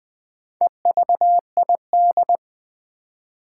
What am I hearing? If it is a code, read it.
Morse code, heard as a keyed tone, about 20 wpm, 700 Hz. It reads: EVID
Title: Build A Word - 2 Letters At A Time - 8-Letter Words - 20wpm